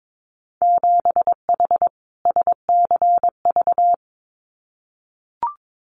Morse code, heard as a keyed tone, about 22 wpm, 700 Hz. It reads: 7H SCV
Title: Winter Field Day Exchanges Encore 22wpm